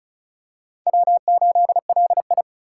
Morse code, heard as a keyed tone, about 34 wpm, 700 Hz. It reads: W8LI
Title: USA Call Signs - 4 Characters - 3x Character Spacing - 35wpm